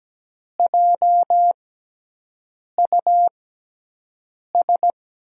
Morse code, heard as a keyed tone, about 17 wpm, spaced as for 6 wpm, 700 Hz. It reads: JUS
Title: USA Call Signs - 6x Character Spacing - 17wpm